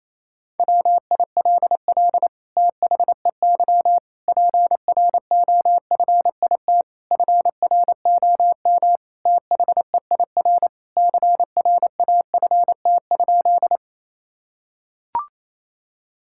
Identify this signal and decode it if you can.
Morse code, heard as a keyed tone, about 28 wpm, 700 Hz. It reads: WILL THEY PROFIT FROM THEIR CRAFT?
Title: Sentences from Top 1000 Words - Review Part 6 - 28wpm